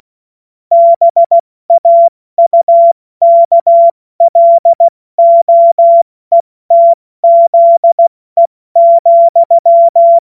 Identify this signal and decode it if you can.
Morse code, heard as a keyed tone, about 16 wpm, 680 Hz. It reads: BAUKLOETZE,
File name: Morsekurs / Witze / Witz 24 Juni Mai 2025 70 F80